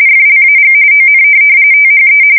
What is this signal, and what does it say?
RTTY, about 45.45 baud. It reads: FLAG ON THE
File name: fsk170 / USS-Missouri